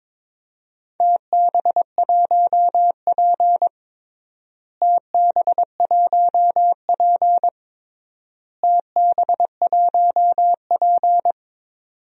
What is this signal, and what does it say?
Morse code, heard as a keyed tone, about 22 wpm, 700 Hz. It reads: TB1P TB1P TB1P
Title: DXCC-100 Call Signs - 4 Characters - Repeated 3x - 22wpm